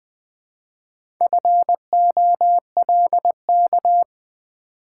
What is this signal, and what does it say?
Morse code, heard as a keyed tone, about 20 wpm, 700 Hz. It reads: FOLK
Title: Build A Word - 1 Letter At A Time - 4-Letter Words - 20wpm